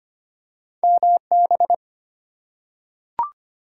Morse code, heard as a keyed tone, about 25 wpm, 700 Hz. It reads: MB
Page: morse-code-ninja-practice-25wpm